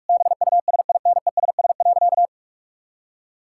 Morse code, heard as a keyed tone, about 41 wpm, 700 Hz. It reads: BUSINESS.
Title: Sentences from Top 1000 Words - Review Part 6 - 45wpm